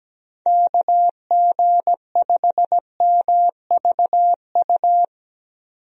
Morse code, heard as a keyed tone, about 17 wpm, 700 Hz. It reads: KG5MVU
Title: USA Call Signs - 6 Characters - 17wpm